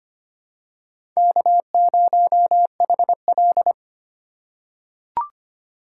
Morse code, heard as a keyed tone, about 25 wpm, 700 Hz. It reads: K0HL
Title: USA Call Signs - 4 Characters - 2x Character Spacing - 25wpm